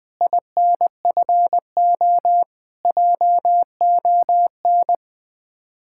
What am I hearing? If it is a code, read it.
Morse code, heard as a keyed tone, about 20 wpm, 700 Hz. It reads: INFO JON
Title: Speed-Racing Common QSO Phrases - 17wpm